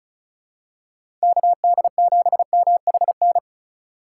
Morse code, heard as a keyed tone, about 34 wpm, 700 Hz. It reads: KD7MHN
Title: USA Call Signs - 6 Characters - 2x Character Spacing - 35wpm